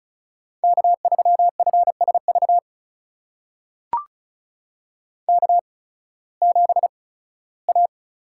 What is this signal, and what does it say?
Morse code, heard as a keyed tone, about 36 wpm, spaced as for 19 wpm, 700 Hz. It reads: K3FSV K7A
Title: USA Call Signs - 5 Characters - 8x Character Spacing - 35wpm